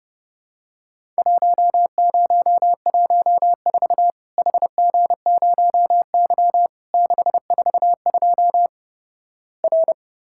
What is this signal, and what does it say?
Morse code, heard as a keyed tone, about 32 wpm, 700 Hz. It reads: 1014 HG0Y 642 R
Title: Worked All Europe DX Contest - QTC Exchange - 30wpm